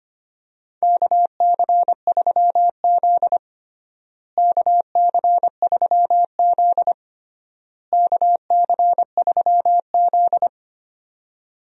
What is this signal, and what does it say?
Morse code, heard as a keyed tone, about 25 wpm, 700 Hz. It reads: KC3Z KC3Z KC3Z
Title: USA Call Signs - 4 Characters - Repeated 3x - 25wpm